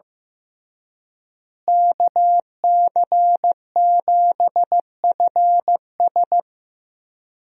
Morse code, heard as a keyed tone, about 15 wpm, 700 Hz. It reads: KC7FS
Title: USA Call Signs - 5 Characters - Repeated 3x - 15wpm